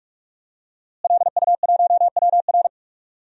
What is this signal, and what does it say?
Morse code, heard as a keyed tone, about 47 wpm, 700 Hz. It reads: LU1WR
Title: DXCC-100 Call Signs - 5 Characters - Repeated 3x - 45wpm